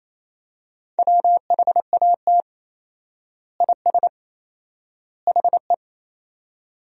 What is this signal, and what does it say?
Morse code, heard as a keyed tone, about 28 wpm, 700 Hz. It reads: WHAT IS HE
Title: 4x Word-Spacing - Sentences from Top 100 Words - 28wpm